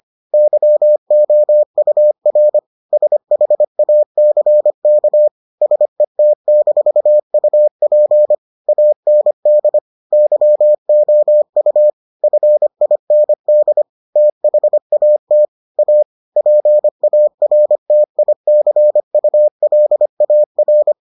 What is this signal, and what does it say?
Morse code, heard as a keyed tone, about 25 wpm, 600 Hz. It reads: YOUR SHACK SET-UP AND YOU FIND THAT A PARTICULAR